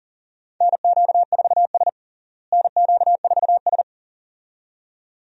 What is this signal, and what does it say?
Morse code, heard as a keyed tone, about 40 wpm, 700 Hz. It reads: NQ4S NQ4S
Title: USA Call Signs - 4 Characters - Repeated 2x - 40wpm